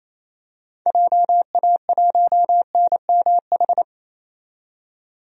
Morse code, heard as a keyed tone, about 28 wpm, 700 Hz. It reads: JA1NMH
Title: DXCC-100 Call Signs - 6 Characters - 28wpm